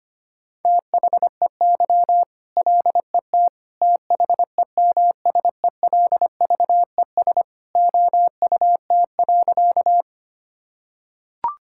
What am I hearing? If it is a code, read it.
Morse code, heard as a keyed tone, about 25 wpm, 700 Hz. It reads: THEY LET THEMSELVES OUT.
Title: Sentences from Top 1000 Words - Review Part 5 - 25wpm